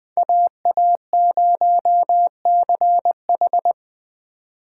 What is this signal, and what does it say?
Morse code, heard as a keyed tone, about 24 wpm, 700 Hz. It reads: AA0CH
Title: Speed Racing - USA 2x2 Call Signs 20wpm